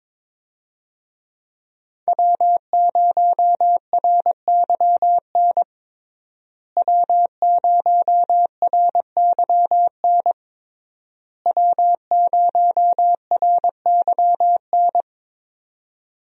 Morse code, heard as a keyed tone, about 21 wpm, 700 Hz. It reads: W0RYN W0RYN W0RYN
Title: USA Call Signs - 5 Characters - Repeated 3x - 22wpm